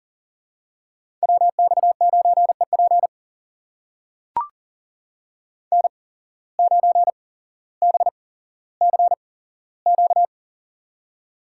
Morse code, heard as a keyed tone, about 43 wpm, spaced as for 17 wpm, 700 Hz. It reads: WX9EP N9BCQ
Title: USA Call Signs - 5 Characters - 8x Character Spacing - 40wpm